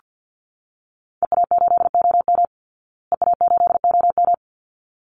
Morse code, heard as a keyed tone, about 48 wpm, 700 Hz. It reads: EA8OM EA8OM
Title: Call Signs Repeated 3x - 50wpm